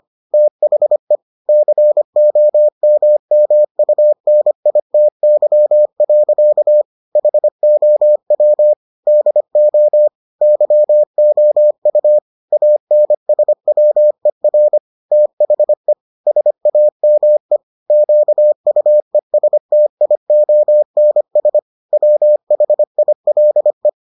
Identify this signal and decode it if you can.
Morse code, heard as a keyed tone, about 25 wpm, 600 Hz. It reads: THE COMMUNITY. HOW DO YOU ANSWER THE SAME QUESTIONS WHILE